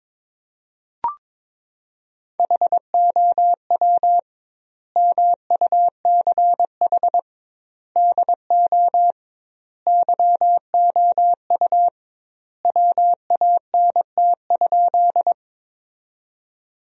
Morse code, heard as a keyed tone, about 22 wpm, 700 Hz. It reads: HOW MUCH DO YOU WANT?
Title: 2x Word-Spacing - Sentences from Top 200 Words - 22wpm